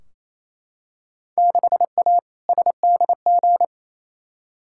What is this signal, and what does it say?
Morse code, heard as a keyed tone, about 28 wpm, 700 Hz. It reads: 6A SDG
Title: Field Day Exchanges Encore - 28wpm